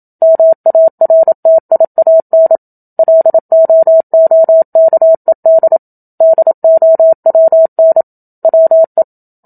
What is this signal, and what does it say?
Morse code, heard as a keyed tone, about 27 wpm, 650 Hz. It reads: MARTIAN LOOKED DOWN WE
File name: Book Two / 01-Chapter 1 - 27 WPM